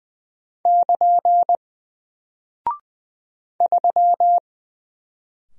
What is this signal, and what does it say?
Morse code, heard as a keyed tone, about 20 wpm, 700 Hz. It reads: <KN> 3
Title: Single Character (Letter:Number:Punctuation:Prosign) Alt 20wpm